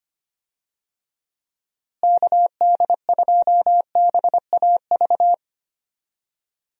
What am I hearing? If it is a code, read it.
Morse code, heard as a keyed tone, about 25 wpm, 700 Hz. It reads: KD2BAV
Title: USA Call Signs - 6 Characters - Repeated 3x - 25wpm